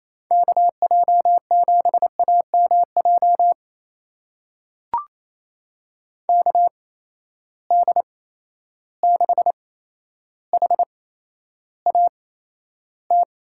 Morse code, heard as a keyed tone, about 28 wpm, spaced as for 12 wpm, 700 Hz. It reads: KJ7AMJ KD6HAT
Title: USA Call Signs - 6 Characters - 8x Character Spacing - 28wpm